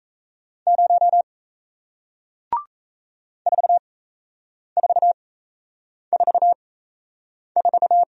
Morse code, heard as a keyed tone, about 38 wpm, spaced as for 8 wpm, 700 Hz. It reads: 0 4444